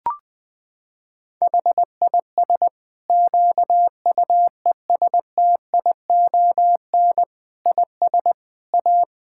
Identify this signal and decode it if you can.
Morse code, heard as a keyed tone, about 20 wpm, 700 Hz. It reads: HIS QUESTION IS A
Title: Sentences from Top 1000 Words - Review Part 5 - 20wpm